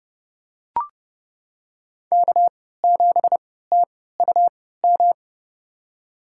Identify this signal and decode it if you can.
Morse code, heard as a keyed tone, about 31 wpm, spaced as for 17 wpm, 700 Hz. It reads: K7TUM
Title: USA Call Signs - 3x Character Spacing - 30wpm